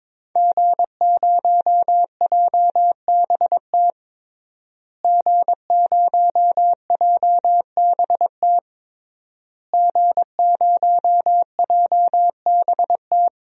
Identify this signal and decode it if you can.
Morse code, heard as a keyed tone, about 22 wpm, 700 Hz. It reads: G0JBT G0JBT G0JBT
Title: DXCC-100 Call Signs - 5 Characters - Repeated 3x - 22wpm